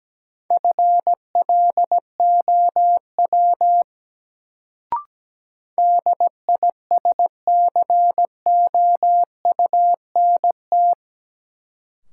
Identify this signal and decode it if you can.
Morse code, heard as a keyed tone, about 17 wpm, 700 Hz. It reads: FLOW DISCOUNT
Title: Sentences from Top 1200 Words - New Word Drill - 17wpm